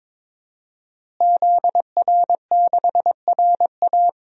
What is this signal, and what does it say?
Morse code, heard as a keyed tone, about 21 wpm, 700 Hz. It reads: ZR6RA
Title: DXCC-100 Call Signs - 5 Characters - Repeated 3x - 22wpm